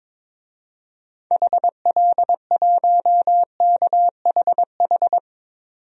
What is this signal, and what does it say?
Morse code, heard as a keyed tone, about 22 wpm, 700 Hz. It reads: HL1KHH
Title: DXCC-100 Call Signs - 6 Characters - 22wpm